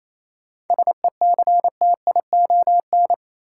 Morse code, heard as a keyed tone, about 28 wpm, 700 Hz. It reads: SECTION
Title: Sentences from Top 1000 Words - Review Part 5 - New Word Drill - 28wpm